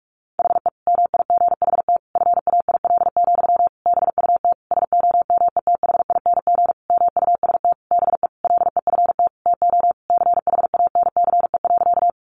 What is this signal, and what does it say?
Morse code, heard as a keyed tone, about 44 wpm, 700 Hz. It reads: HE MIGHT FAIL, BUT SOMETHING MUST BE LEFT TO CHANCE.
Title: Sentences from Top 1100 Words 45wpm